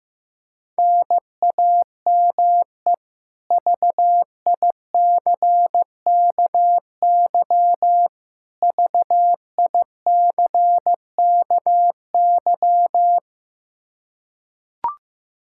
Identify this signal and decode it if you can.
Morse code, heard as a keyed tone, about 15 wpm, 700 Hz. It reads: NAME VICKY VICKY
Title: K1USN SST QSO Practice 15wpm